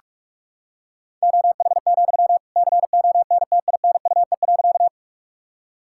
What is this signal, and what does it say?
Morse code, heard as a keyed tone, about 46 wpm, 700 Hz. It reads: OH, CONTINUE.
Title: Sentences from Top 1000 Words - Review Part 6 - 45wpm